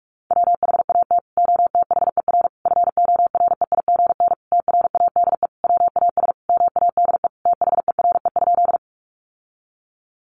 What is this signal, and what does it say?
Morse code, heard as a keyed tone, about 44 wpm, 700 Hz. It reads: WHAT OTHER FOREIGN TRADE WAS MADE THERE?
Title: Sentences from Top 1400 Words 45wpm